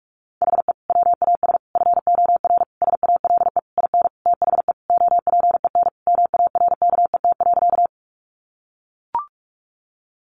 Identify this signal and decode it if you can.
Morse code, heard as a keyed tone, about 44 wpm, 700 Hz. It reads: HE WAS FOR SALE IN THE OPEN MARKET.
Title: Sentences from Top 1400 Words 45wpm